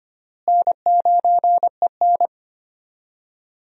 Morse code, heard as a keyed tone, about 25 wpm, 700 Hz. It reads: N9EN
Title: USA Call Signs - 4 Characters - Repeated 3x - 25wpm